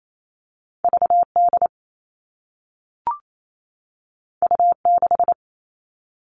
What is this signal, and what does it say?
Morse code, heard as a keyed tone, about 28 wpm, 700 Hz. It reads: VD U6
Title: All Cut Numbers - 2-Digit 28wpm